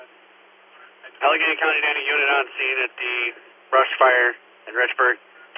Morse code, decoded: none

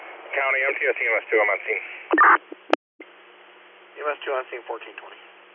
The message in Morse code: none

